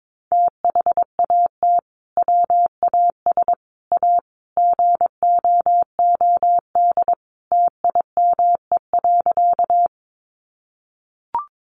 Morse code, heard as a keyed tone, about 22 wpm, 700 Hz. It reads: THAT WAS A GOOD TIME.
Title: Sentences from Top 200 Words 22wpm